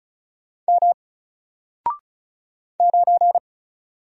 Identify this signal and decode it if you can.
Morse code, heard as a keyed tone, about 36 wpm, 700 Hz. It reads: M 9
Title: Single Character (Letter:Number:Punctuation:Prosign) 35wpm